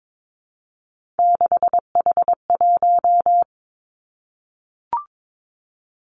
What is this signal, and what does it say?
Morse code, heard as a keyed tone, about 22 wpm, 700 Hz. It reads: 6H1
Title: Three Challenging Letter-Number Combinations 22wpm